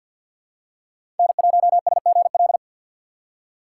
Morse code, heard as a keyed tone, about 52 wpm, 700 Hz. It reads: N1SGL
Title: USA Call Signs - 5 Characters - 4x Character Spacing - 50wpm